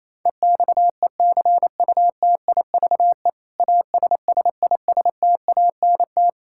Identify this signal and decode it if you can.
Morse code, heard as a keyed tone, about 28 wpm, 700 Hz. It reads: EXECUTIVE ASSISTANT
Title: Practice QSO - Exchange 3 - More Info - 28wpm